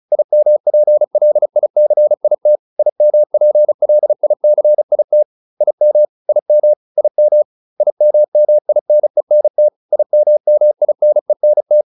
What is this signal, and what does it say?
Morse code, heard as a keyed tone, about 34 wpm, 600 Hz. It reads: IMPLICIT IMPLICIT IM IM IM IMMINENT IMMINENT